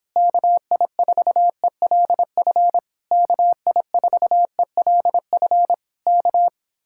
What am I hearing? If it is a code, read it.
Morse code, heard as a keyed tone, about 26 wpm, 700 Hz. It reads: KI4ELF KI4ELF K